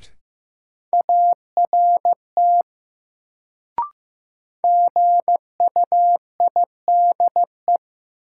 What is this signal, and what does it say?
Morse code, heard as a keyed tone, about 16 wpm, 700 Hz. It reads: ART GUIDE
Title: Sentences from Top 600 Words - New Word Drill - 15wpm